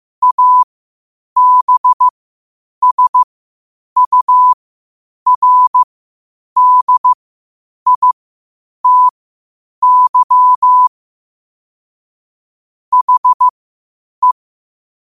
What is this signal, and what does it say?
Morse code, heard as a keyed tone, about 15 wpm, spaced as for 8 wpm, 1000 Hz. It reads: ABSURDITY HE
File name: qotdfiles / 8wpm